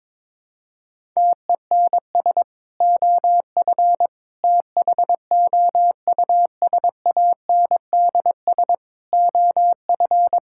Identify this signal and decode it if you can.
Morse code, heard as a keyed tone, about 22 wpm, 700 Hz. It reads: TENS OF THOUSANDS OF
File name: practice / 22wpm / Sentences with 4-Word Phrases - 22wpm